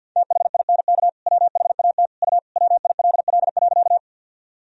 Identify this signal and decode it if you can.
Morse code, heard as a keyed tone, about 53 wpm, 700 Hz. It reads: THINK WHAT U WILL.
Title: Sentences from Top 200 Words and Abbreviations 50wpm